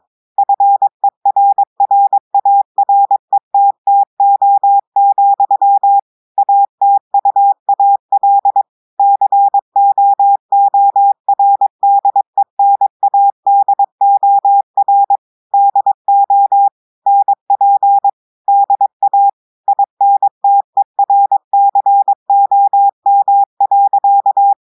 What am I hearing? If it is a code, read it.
Morse code, heard as a keyed tone, about 22 wpm, 800 Hz. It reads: FERRARETTO, ATUAL COORDENADOR DO NP DA INTERCOM.
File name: audios / portugais / portugais texte en clair 1 vitesse 22 mots minutes